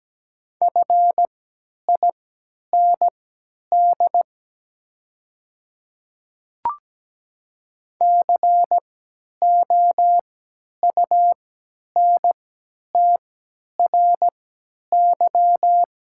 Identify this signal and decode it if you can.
Morse code, heard as a keyed tone, about 17 wpm, spaced as for 10 wpm, 690 Hz. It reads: FIND COUNTRY
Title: 3x Character Spacing - Top 300 Words - 17wpm